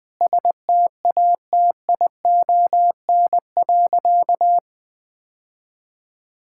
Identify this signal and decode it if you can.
Morse code, heard as a keyed tone, about 20 wpm, 700 Hz. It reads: STATION.
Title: Difficult Sentences with Hints - 20wpm